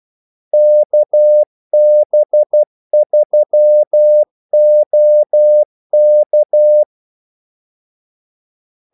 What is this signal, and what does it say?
Morse code, heard as a keyed tone, about 12 wpm, 600 Hz. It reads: KB3OK
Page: 2 ekstra ordmellemrum mellem kaldesignalerne